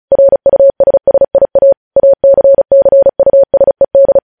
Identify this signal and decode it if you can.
Morse code, heard as a keyed tone, about 36 wpm, 550 Hz. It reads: RUSSIA ACCUSED
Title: News Headlines at 35